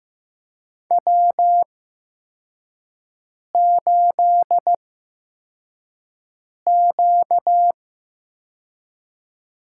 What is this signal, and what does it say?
Morse code, heard as a keyed tone, about 15 wpm, spaced as for 4 wpm, 700 Hz. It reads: W8Q
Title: USA Call Signs - 4 Characters - 8x Character Spacing - 15wpm